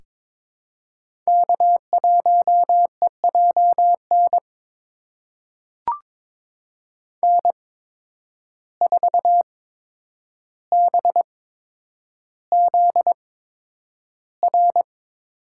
Morse code, heard as a keyed tone, about 22 wpm, spaced as for 9 wpm, 700 Hz. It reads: K1EJN N4BZR
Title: USA Call Signs - 5 Characters - 8x Character Spacing - 22wpm